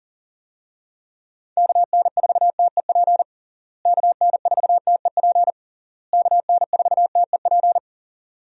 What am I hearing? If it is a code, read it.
Morse code, heard as a keyed tone, about 40 wpm, 700 Hz. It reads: KN4TEP KN4TEP KN4TEP
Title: USA Call Signs - 6 Characters - Repeated 3x - 40wpm